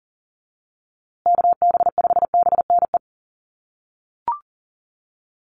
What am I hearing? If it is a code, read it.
Morse code, heard as a keyed tone, about 40 wpm, 700 Hz. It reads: KB5BNE
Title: USA Call Signs Repeated 3x 40wpm